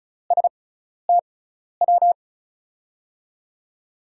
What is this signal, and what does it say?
Morse code, heard as a keyed tone, about 36 wpm, spaced as for 12 wpm, 700 Hz. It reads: STW
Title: USA Call Signs - 5 Characters - 6x Character Spacing - 35wpm